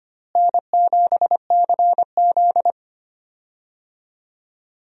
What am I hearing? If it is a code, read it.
Morse code, heard as a keyed tone, about 25 wpm, 700 Hz. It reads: N7CZ
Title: USA Call Signs - 4 Characters - Repeated 3x - 25wpm